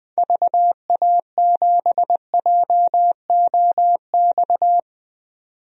Morse code, heard as a keyed tone, about 20 wpm, 700 Hz. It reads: VA7JOX
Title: DXCC-100 Call Signs - 6 Characters - 20wpm